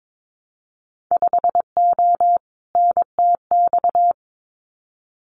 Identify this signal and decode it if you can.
Morse code, heard as a keyed tone, about 22 wpm, 700 Hz. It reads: 5O NTX
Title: Winter Field Day Exchanges Encore 22wpm